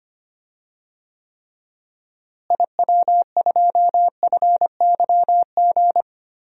Morse code, heard as a keyed tone, about 25 wpm, 700 Hz. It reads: IW2FYG
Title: DXCC-100 Call Signs - 6 Characters - Repeated 3x - 25wpm